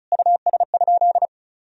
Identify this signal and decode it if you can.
Morse code, heard as a keyed tone, about 36 wpm, 700 Hz. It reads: US?
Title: Sentences from Top 1000 Words - Review Part 4 - 35wpm